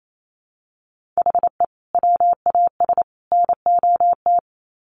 Morse code, heard as a keyed tone, about 28 wpm, 700 Hz. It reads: HE WAS NOT
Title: Sets of 3 Words 28wpm